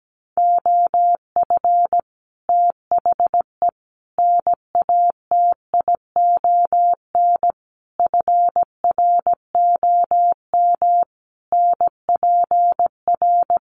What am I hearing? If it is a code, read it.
Morse code, heard as a keyed tone, about 17 wpm, 700 Hz. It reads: OF THE NATION FROM NPR
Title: Sets of 5 Words 17wpm